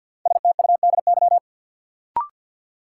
Morse code, heard as a keyed tone, about 48 wpm, 700 Hz. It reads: STUDY
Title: Sets of 5 Words - New Word Drill - 50wpm